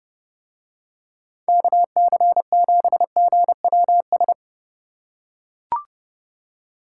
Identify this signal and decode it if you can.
Morse code, heard as a keyed tone, about 31 wpm, 700 Hz. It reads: KC7GWS